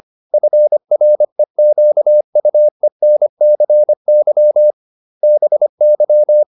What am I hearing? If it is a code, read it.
Morse code, heard as a keyed tone, about 25 wpm, 600 Hz. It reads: FREQUENCY BY